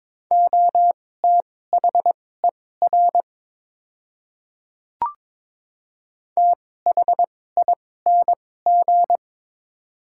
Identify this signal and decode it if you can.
Morse code, heard as a keyed tone, about 22 wpm, spaced as for 16 wpm, 700 Hz. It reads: OTHER THING
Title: 2x Character Spacing - Top 300 Words - 22wpm